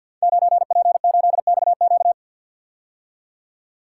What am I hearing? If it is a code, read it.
Morse code, heard as a keyed tone, about 52 wpm, 700 Hz. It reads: 9P8XQ
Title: Code Groups - 5 Letter:Numbers - 50wpm